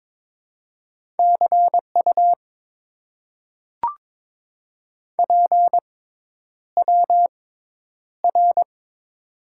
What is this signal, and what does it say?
Morse code, heard as a keyed tone, about 22 wpm, spaced as for 10 wpm, 700 Hz. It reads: CU PWR
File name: Top 90 QSO Elements - 6x Character Spacing - 22wpm